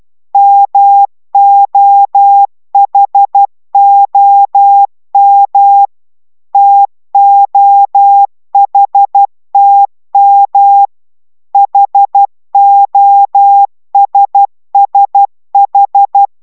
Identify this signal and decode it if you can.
Morse code, heard as a keyed tone, about 12 wpm, 800 Hz. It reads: MOHOM TOHTM HOSSH